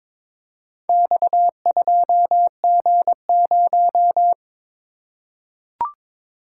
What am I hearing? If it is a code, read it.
Morse code, heard as a keyed tone, about 22 wpm, 700 Hz. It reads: X2G0
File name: practice / 22wpm / Four Letter-Number Combinations - 22wpm